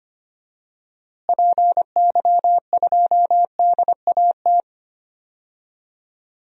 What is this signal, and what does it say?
Morse code, heard as a keyed tone, about 25 wpm, 700 Hz. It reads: PY2DAT
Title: DXCC-100 Call Signs - 6 Characters - Repeated 3x - 25wpm